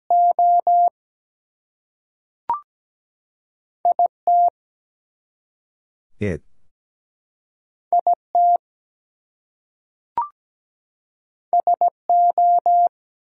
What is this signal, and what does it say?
Morse code, heard as a keyed tone, about 17 wpm, 700 Hz. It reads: O IT IT SO
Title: Two Letter Words - 17wpm